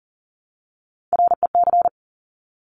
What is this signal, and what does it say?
Morse code, heard as a keyed tone, about 40 wpm, 700 Hz. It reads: REC
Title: Top Three Letter Combinations in Words 40wpm